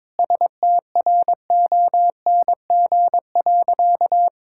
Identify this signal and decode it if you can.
Morse code, heard as a keyed tone, about 21 wpm, 700 Hz. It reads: STRONG.
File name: practice / 22wpm / Sentences with 8-Word Phrases - 22wpm